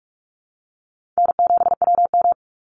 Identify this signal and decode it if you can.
Morse code, heard as a keyed tone, about 44 wpm, 690 Hz. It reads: N7WM
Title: NA SOTA Chaser Callsigns 45wpm